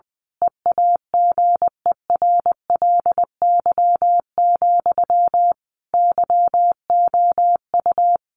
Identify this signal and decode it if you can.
Morse code, heard as a keyed tone, about 20 wpm, 700 Hz. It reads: EAGERLY, YOU
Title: A Sound Of Thunder 20wpm